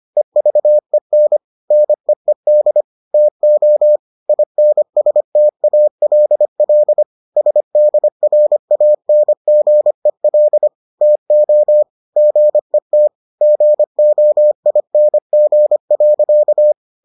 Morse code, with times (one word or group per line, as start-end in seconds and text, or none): 0.2-1.4 EVEN
1.7-2.8 NEED
3.1-4.0 TO
4.3-7.0 INSTALL
7.4-10.7 SDRANGEL
11.0-11.8 TO
12.2-13.1 GET
13.4-16.7 GOING.